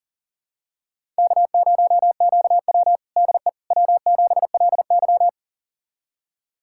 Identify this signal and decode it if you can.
Morse code, heard as a keyed tone, about 40 wpm, 700 Hz. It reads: K0QW DE W7LY
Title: USA Callsign DE Callsign - 4 Characters - 40wpm